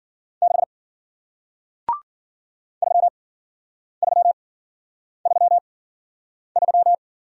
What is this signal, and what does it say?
Morse code, heard as a keyed tone, about 53 wpm, spaced as for 9 wpm, 700 Hz. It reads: 6 3333